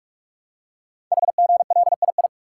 Morse code, heard as a keyed tone, about 44 wpm, 700 Hz. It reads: HGLII
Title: Code Groups - 5 Letters - 45wpm